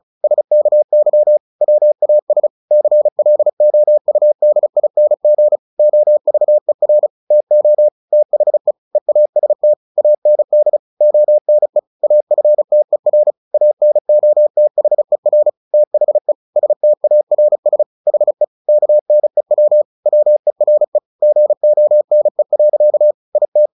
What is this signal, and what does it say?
Morse code, heard as a keyed tone, about 35 wpm, 600 Hz. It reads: SKY WAS CLOUDING OVER TO THE EAST AND ONE AFTER ANOTHER THE STARS HE KNEW WERE GONE. IT
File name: the old man and the sea 2 35wpm